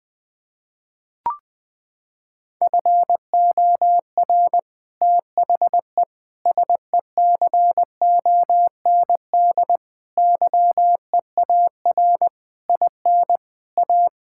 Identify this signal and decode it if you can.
Morse code, heard as a keyed tone, about 20 wpm, 700 Hz. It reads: FOR THE SECOND YEAR IN A
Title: Sets of 8 Words - 20wpm